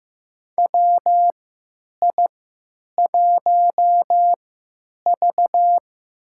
Morse code, heard as a keyed tone, about 15 wpm, spaced as for 9 wpm, 700 Hz. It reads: WI1V
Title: USA Call Signs - 4 Characters - 3x Character Spacing - 15wpm